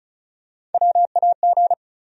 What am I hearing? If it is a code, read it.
Morse code, heard as a keyed tone, about 36 wpm, 700 Hz. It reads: WAG
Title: USA Call Sign Postfixes - 3 Letters 35wpm